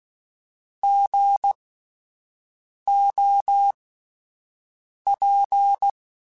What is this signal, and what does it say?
Morse code, heard as a keyed tone, about 16 wpm, spaced as for 5 wpm, 800 Hz. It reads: GOP
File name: Alphabetical / Alphabetical mp3s / Gs @5